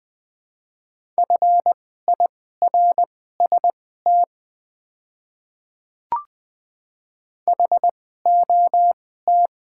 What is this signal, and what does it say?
Morse code, heard as a keyed tone, about 20 wpm, spaced as for 15 wpm, 700 Hz. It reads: FIRST HOT
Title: 2x Character Spacing - Top 100 Words - 20wpm